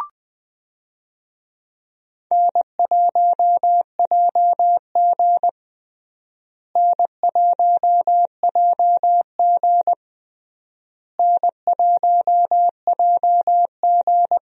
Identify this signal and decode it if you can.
Morse code, heard as a keyed tone, about 21 wpm, 700 Hz. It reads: N1JG N1JG N1JG
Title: USA Call Signs - 4 Characters - Repeated 3x - 20wpm